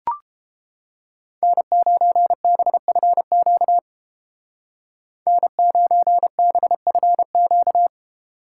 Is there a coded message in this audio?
Yes: Morse code from 1.4 to 7.9 s, heard as a keyed tone, about 31 wpm, 700 Hz. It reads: N9BFQ N9BFQ